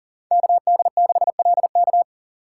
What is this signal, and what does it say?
Morse code, heard as a keyed tone, about 37 wpm, 700 Hz. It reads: KD6LK
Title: USA Call Signs - 5 Characters - 6x Character Spacing - 40wpm